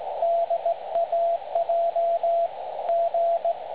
Morse code, heard as a keyed tone, about 18 wpm, 680 Hz, with radio noise. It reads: DAJ G
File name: dlazaawansowanych / nagraniazpasma / websdr recording 3565.0kHz